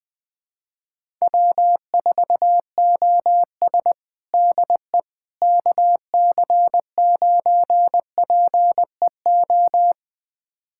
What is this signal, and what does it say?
Morse code, heard as a keyed tone, about 20 wpm, 690 Hz. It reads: W4OS DE KC9PEO
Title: USA Callsign DE Callsign 20wpm